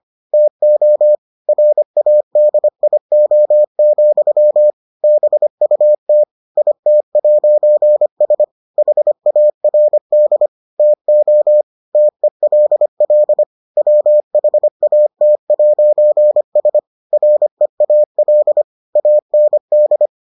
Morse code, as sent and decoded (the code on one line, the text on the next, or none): - --- / .-. .- -.. .. --- --..-- / -... ..- - / .. - .----. ... / .... .- .-. -.. / - --- / - . .-.. .-.. / .-- .... .- - .----. ... / .-. .- .-.. / .- -. -..
TO RADIO, BUT IT'S HARD TO TELL WHAT'S REAL AND